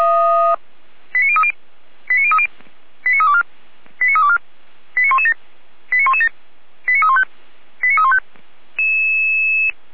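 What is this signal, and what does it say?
Five-tone selective call (ZVEI), heard as quick runs of five tones, tones of about 70 ms. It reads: 89030 89030 89324 89324 89117 89117 89315 89315
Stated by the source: ZVEI five-tone call received on SDR.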